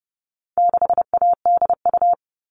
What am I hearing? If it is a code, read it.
Morse code, heard as a keyed tone, about 30 wpm, 700 Hz. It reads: 6ADU
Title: All Cut Numbers - 4-Digit 30wpm